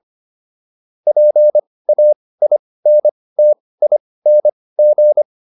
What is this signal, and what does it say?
Morse code, heard as a keyed tone, about 25 wpm, spaced as for 18 wpm, 600 Hz. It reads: PAINTING